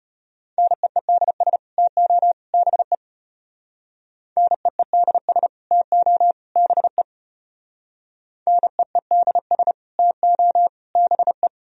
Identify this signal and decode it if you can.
Morse code, heard as a keyed tone, about 30 wpm, 700 Hz. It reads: NEEDS TO BE NEEDS TO BE NEEDS TO BE
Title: Speed-Racing - Sets of 3 Words 25wpm